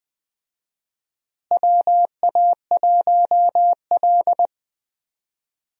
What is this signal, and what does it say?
Morse code, heard as a keyed tone, about 21 wpm, 700 Hz. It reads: WA1L
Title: USA Call Signs - 4 Characters - Repeated 3x - 20wpm